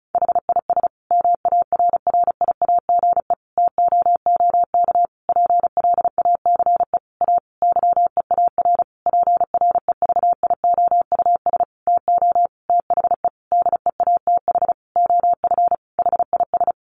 Morse code, heard as a keyed tone, about 34 wpm, 700 Hz. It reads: HIS MARRIAGE TOOK PLACE A YEAR PREVIOUS TO THE DEATH OF HIS
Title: Sentences from Top 1500 Words 35wpm